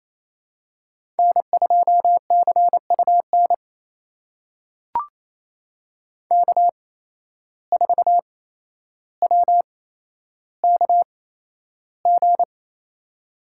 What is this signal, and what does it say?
Morse code, heard as a keyed tone, about 28 wpm, spaced as for 12 wpm, 700 Hz. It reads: N2CUN K4WKG
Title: USA Call Signs - 5 Characters - 8x Character Spacing - 28wpm